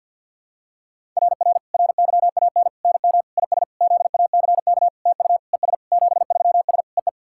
Morse code, heard as a keyed tone, about 50 wpm, 700 Hz. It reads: RR RYAN NM IS ZACK TU ES 73S EE
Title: SOTA QSO Practice 50wpm